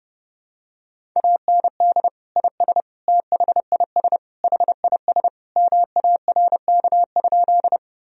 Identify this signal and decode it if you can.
Morse code, heard as a keyed tone, about 30 wpm, 700 Hz. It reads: AND IS THIS HIS MARK?
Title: Sentences from Top 1000 Words - Review Part 1 - 30wpm